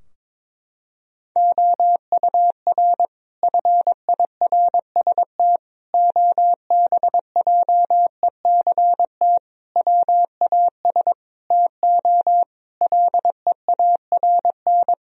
Morse code, heard as a keyed tone, about 22 wpm, 700 Hz. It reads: OUR FIRST OBJECT WAS TO LEARN
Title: Sentences from Top 1000 Words - Review Part 3 - 22wpm